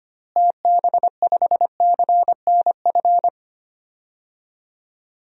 Morse code, heard as a keyed tone, about 25 wpm, 700 Hz. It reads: TB5CNF
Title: DXCC-100 Call Signs - 6 Characters - Repeated 2x - 25wpm